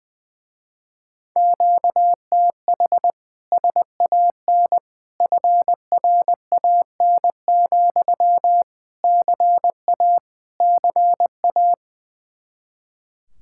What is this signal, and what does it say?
Morse code, heard as a keyed tone, about 20 wpm, 700 Hz. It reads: QTH SAN FRAN, CA CA
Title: Speed-Racing Common QSO Phrases Encore - 20wpm